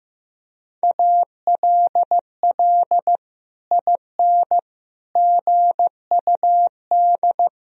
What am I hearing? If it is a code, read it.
Morse code, heard as a keyed tone, about 16 wpm, 700 Hz. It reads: ALL IN GUD
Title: Sentences from Top 200 Words and Abbreviations 15wpm